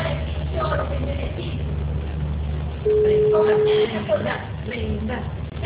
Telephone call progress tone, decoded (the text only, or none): ringback tone